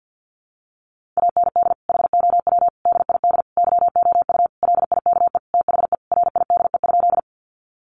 Morse code, heard as a keyed tone, about 50 wpm, 700 Hz. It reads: AND HOW DID YOU LIKE THE RIDE?
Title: Sentences from Top 1200 Words 50wpm